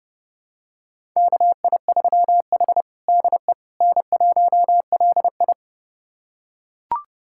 Morse code, heard as a keyed tone, about 29 wpm, 700 Hz. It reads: KI3H DE N1LI
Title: USA Callsign DE Callsign - 4 Characters - 30wpm